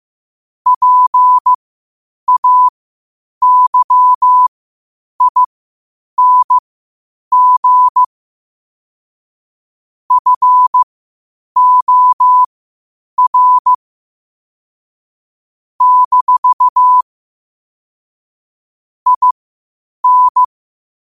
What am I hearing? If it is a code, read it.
Morse code, heard as a keyed tone, about 15 wpm, spaced as for 8 wpm, 1000 Hz. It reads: PAYING FOR - IN
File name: qotdfiles / 8wpm